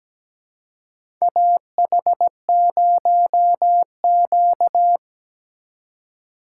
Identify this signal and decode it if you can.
Morse code, heard as a keyed tone, about 17 wpm, 700 Hz. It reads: AH0Q
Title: USA Call Signs - 4 Characters - 2x Character Spacing - 17wpm